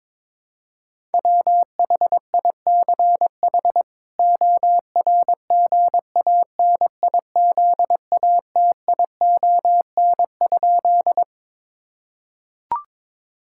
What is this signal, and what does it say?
Morse code, heard as a keyed tone, about 21 wpm, 700 Hz. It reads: WHICH ORGANIZATION?
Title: Sentences from Top 1000 Words - Review Part 5 - 22wpm